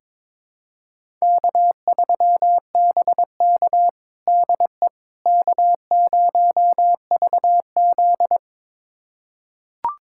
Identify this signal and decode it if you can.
Morse code, heard as a keyed tone, about 22 wpm, 700 Hz. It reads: K3BK DE K0VZ
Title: USA Callsign DE Callsign - 4 Characters - 22wpm